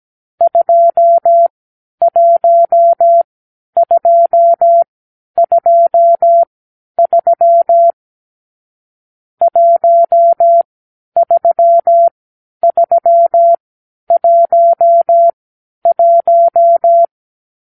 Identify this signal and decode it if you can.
Morse code, heard as a keyed tone, about 17 wpm, spaced as for 10 wpm, 680 Hz. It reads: 21223 13311
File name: k7qo / Lesson 44 (1-3)